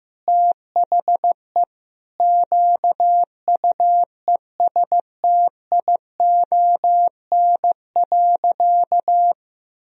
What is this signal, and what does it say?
Morse code, heard as a keyed tone, about 15 wpm, 700 Hz. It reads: THE QUESTION.
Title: Sentences from Top 1000 Words - Review Part 4 - 15wpm